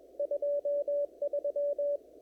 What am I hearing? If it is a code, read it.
Morse code, heard as a keyed tone, about 22 wpm, 580 Hz, with radio noise. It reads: 23
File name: CW / kt5j wpx cw-op-k5pi